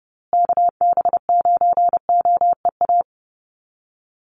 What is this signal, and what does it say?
Morse code, heard as a keyed tone, about 30 wpm, 700 Hz. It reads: KB9OEA
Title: USA Call Signs Repeated 2x 30wpm